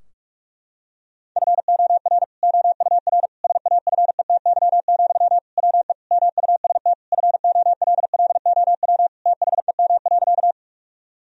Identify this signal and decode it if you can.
Morse code, heard as a keyed tone, about 42 wpm, 700 Hz. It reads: FOR OUR SAFETY, WE MUST FOLLOW THEM.
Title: Sentences from Top 1000 Words - Review Part 4 - 45wpm